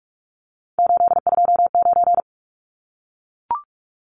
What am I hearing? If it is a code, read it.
Morse code, heard as a keyed tone, about 44 wpm, 700 Hz. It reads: 829
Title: Numbers - 3-Digit 45wpm